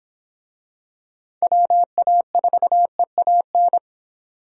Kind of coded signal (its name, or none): Morse code